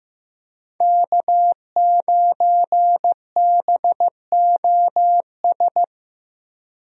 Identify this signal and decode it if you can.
Morse code, heard as a keyed tone, about 15 wpm, 700 Hz. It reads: K9BOS
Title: USA Call Signs - 5 Characters - Repeated 3x - 15wpm